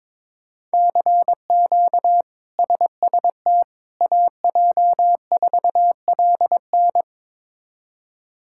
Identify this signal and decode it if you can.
Morse code, heard as a keyed tone, about 22 wpm, 700 Hz. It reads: CQ SST AJ4LN
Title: K1USN SST QSO Practice - 22wpm